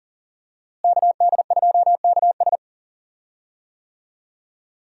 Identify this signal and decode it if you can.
Morse code, heard as a keyed tone, about 40 wpm, 700 Hz. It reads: KD2KS
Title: USA Call Signs - 5 Characters - Repeated 3x - 40wpm